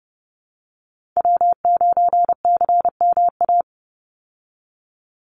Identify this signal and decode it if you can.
Morse code, heard as a keyed tone, about 30 wpm, 700 Hz. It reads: W9CMA
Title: USA Call Signs 30wpm